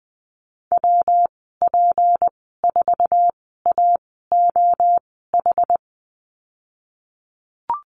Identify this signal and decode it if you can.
Morse code, heard as a keyed tone, about 20 wpm, spaced as for 15 wpm, 700 Hz. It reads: WP4AOH
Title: 186 Lesson 30g - Letter b - Call Signs 20wpm